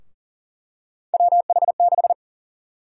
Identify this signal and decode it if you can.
Morse code, heard as a keyed tone, about 40 wpm, 700 Hz. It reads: WH6
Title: USA Call Sign Prefixes - Repeated 2x - 40wpm